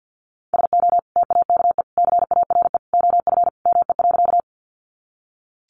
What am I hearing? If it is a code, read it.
Morse code, heard as a keyed tone, about 50 wpm, 700 Hz. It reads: SO TAKE CARE OF ME.
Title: Sentences from Top 300 Words 50wpm